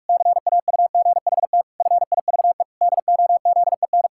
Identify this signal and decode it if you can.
Morse code, heard as a keyed tone, about 48 wpm, 700 Hz. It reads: CAUGHT FIVE DOZEN
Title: Pangram Challenge v2 - 45wpm